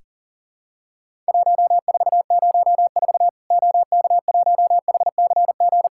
Morse code, heard as a keyed tone, about 40 wpm, 700 Hz. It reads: 1404 OK1HCG
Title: Worked All Europe DX Contest - QTC Exchange - 40wpm